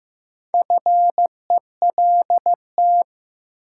Morse code, heard as a keyed tone, about 15 wpm, 700 Hz. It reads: FELT
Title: Sets of 5 Words Encore - New Word Drill - 15wpm